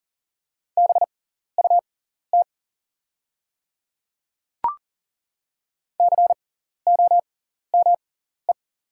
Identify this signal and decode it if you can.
Morse code, heard as a keyed tone, about 40 wpm, spaced as for 14 wpm, 700 Hz. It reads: BUT COME
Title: 6x Character Spacing - Top 100 Words - 40wpm